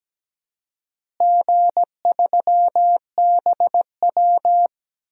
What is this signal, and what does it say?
Morse code, heard as a keyed tone, about 17 wpm, 700 Hz. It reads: G3BW
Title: DXCC-100 Call Signs - 4 Characters - Repeated 3x - 17wpm